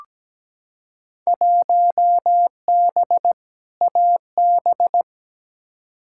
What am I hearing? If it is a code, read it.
Morse code, heard as a keyed tone, about 17 wpm, 700 Hz. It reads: 1B AB
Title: Field Day Exchanges - 17wpm